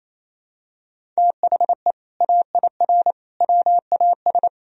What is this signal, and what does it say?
Morse code, heard as a keyed tone, about 28 wpm, 700 Hz. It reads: THE AIR WAS
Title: Sentences from Top 1000 Words - Review Part 3 - 28wpm